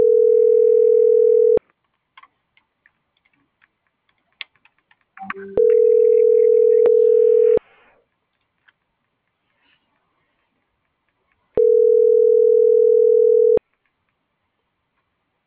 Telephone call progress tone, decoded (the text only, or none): ringback tone